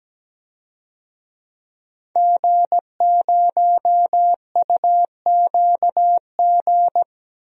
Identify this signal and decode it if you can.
Morse code, heard as a keyed tone, about 17 wpm, 700 Hz. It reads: G0UQG
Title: DXCC-100 Call Signs - 5 Characters - Repeated 2x - 17wpm